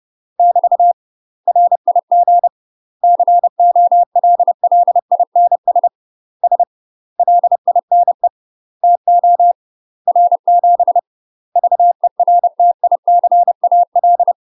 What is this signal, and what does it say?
Morse code, heard as a keyed tone, about 30 wpm, 700 Hz. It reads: <BT> RIG COLLINS S LINE TO R7 VERTICAL
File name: UK MASTER QSOs TRAINING / 30WPM / UK MASTER QSOs 0023